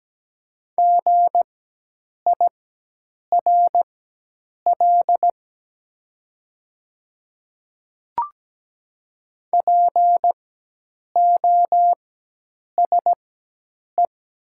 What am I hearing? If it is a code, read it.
Morse code, heard as a keyed tone, about 17 wpm, spaced as for 8 wpm, 700 Hz. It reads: GIRL POSE
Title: 4x Character Spacing - Top 500 Words - 17wpm